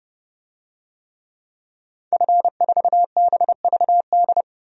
Morse code, heard as a keyed tone, about 32 wpm, 700 Hz. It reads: F4BVD
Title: DXCC-100 Call Signs - 5 Characters - 30wpm